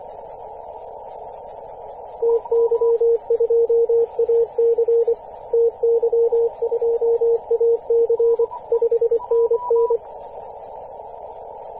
Morse code, heard as a keyed tone, about 24 wpm, 460 Hz, with radio noise. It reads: TY2AC TY2AC 5NN
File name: t32fs-10mhz-cw